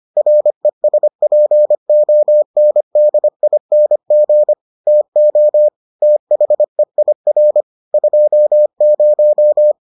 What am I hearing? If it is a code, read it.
Morse code, heard as a keyed tone, about 25 wpm, 600 Hz. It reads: RESPONDING TO THEIR 20